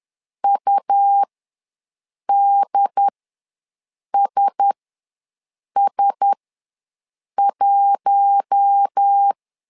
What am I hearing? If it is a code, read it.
Morse code, heard as a keyed tone, about 10 wpm, spaced as for 6 wpm, 800 Hz. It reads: UDSS1
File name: LEZIONI / 4 Q1S ws 10wpm 2208